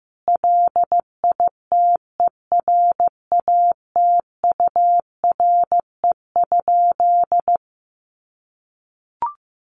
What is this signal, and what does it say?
Morse code, heard as a keyed tone, about 15 wpm, 700 Hz. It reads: LITERATURE?
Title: Sentences from Top 600 Words 15wpm